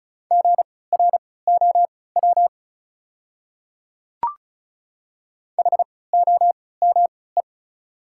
Morse code, heard as a keyed tone, about 34 wpm, spaced as for 20 wpm, 700 Hz. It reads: GROW HOME
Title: 3x Character Spacing - Top 500 Words - 35wpm